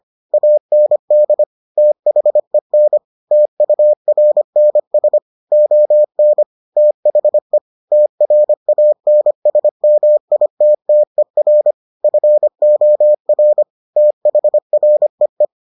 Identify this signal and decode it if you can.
Morse code, heard as a keyed tone, about 25 wpm, 600 Hz. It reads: AND THEN TURNS ON THE TRANSMITTER FOR THREE